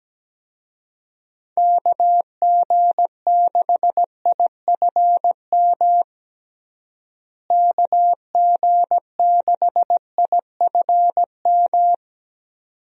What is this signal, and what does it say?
Morse code, heard as a keyed tone, about 17 wpm, 700 Hz. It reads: KG6IFM KG6IFM
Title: USA Call Signs - 6 Characters - Repeated 2x - 17wpm